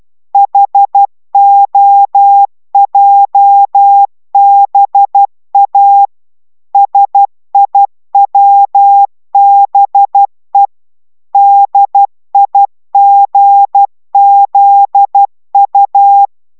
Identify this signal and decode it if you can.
Morse code, heard as a keyed tone, about 12 wpm, 800 Hz. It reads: HOJBA SIWBE DIGZU